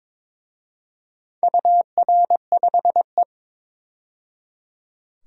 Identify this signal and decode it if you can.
Morse code, heard as a keyed tone, about 22 wpm, 700 Hz. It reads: UR5E
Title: DXCC-100 Call Signs - 4 Characters - 22wpm